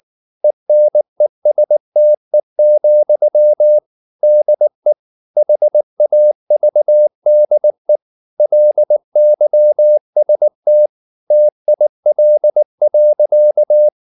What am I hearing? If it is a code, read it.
Morse code, heard as a keyed tone, about 19 wpm, 600 Hz. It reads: ENESTE, DE HAVDE LYST TIL.
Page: Ferie 19 wpm